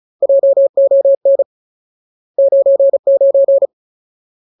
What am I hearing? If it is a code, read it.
Morse code, heard as a keyed tone, about 34 wpm, 550 Hz. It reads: JON 99
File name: Ultimate Word List / 35WPM Ultimate